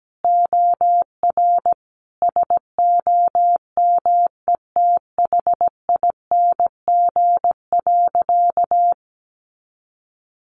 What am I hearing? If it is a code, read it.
Morse code, heard as a keyed tone, about 17 wpm, 700 Hz. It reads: OR SOMETHING.
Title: Sentences from Top 1000 Words 17wpm